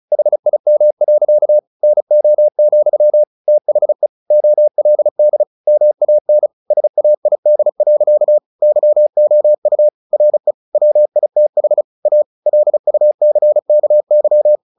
Morse code, heard as a keyed tone, about 34 wpm, 600 Hz. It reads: HIM. NO, THE OLD MAN SAID. YOU RE WITH A LUCKY
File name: the old man and the sea / the old man and the sea 1 35wpm